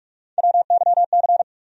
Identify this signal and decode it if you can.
Morse code, heard as a keyed tone, about 46 wpm, 700 Hz. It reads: WYC
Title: USA Call Sign Postfixes - 3 Letters 45wpm